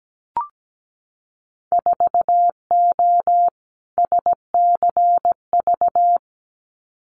Morse code, heard as a keyed tone, about 17 wpm, 700 Hz. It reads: 4O SCV
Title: Winter Field Day Exchanges 17wpm